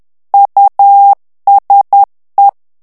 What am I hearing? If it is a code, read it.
Morse code, heard as a keyed tone, about 10 wpm, 800 Hz. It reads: USE
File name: ANGLAIS texte clair 3 vitesse 10 mots minute